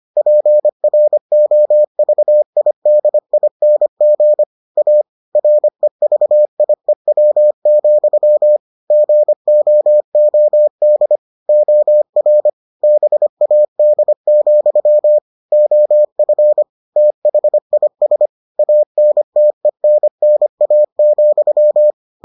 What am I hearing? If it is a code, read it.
Morse code, heard as a keyed tone, about 25 wpm, 600 Hz. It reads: PROVIDING A REVIEW, GOOD OR BAD, OF THIS ANTENNA,